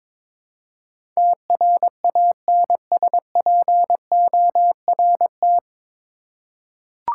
Morse code, heard as a keyed tone, about 22 wpm, 700 Hz. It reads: TRANSPORT
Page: Morse Code Ninja practice 22wpm